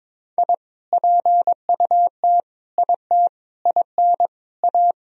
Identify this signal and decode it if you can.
Morse code, heard as a keyed tone, about 22 wpm, 700 Hz. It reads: I PUT IT IN A
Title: Sentences from Top 1000 Words - Review Part 2 - 22wpm